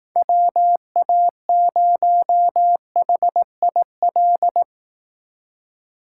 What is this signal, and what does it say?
Morse code, heard as a keyed tone, about 18 wpm, 700 Hz. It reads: WA0HIL
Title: Speed Racing - USA Call Signs 15wpm